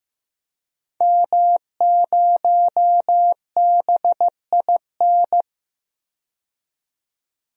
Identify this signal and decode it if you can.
Morse code, heard as a keyed tone, about 15 wpm, 700 Hz. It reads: M0BIN
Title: DXCC-100 Call Signs - 5 Characters - 15wpm